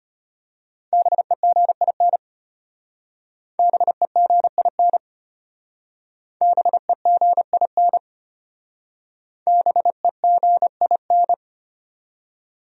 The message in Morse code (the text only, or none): BEGIN BEGIN BEGIN BEGIN